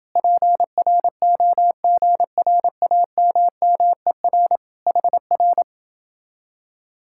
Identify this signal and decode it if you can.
Morse code, heard as a keyed tone, about 28 wpm, 700 Hz. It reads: PROGRAMMER HR